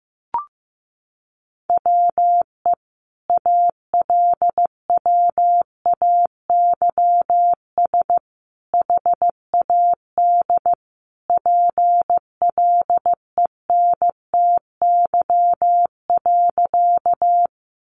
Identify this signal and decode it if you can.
Morse code, heard as a keyed tone, about 15 wpm, 700 Hz. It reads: WE ALWAYS HAD PLENTY.
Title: Sentences from Top 1300 Words 15wpm